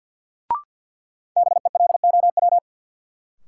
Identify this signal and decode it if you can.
Morse code, heard as a keyed tone, about 50 wpm, 700 Hz. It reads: BELOW
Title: Sentences from Top 1000 Words - Review Part 5 - New Word Drill - 50wpm